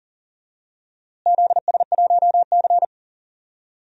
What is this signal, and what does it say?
Morse code, heard as a keyed tone, about 40 wpm, 700 Hz. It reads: ZS1C
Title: DXCC-100 Call Signs - 4 Characters - Repeated 3x - 40wpm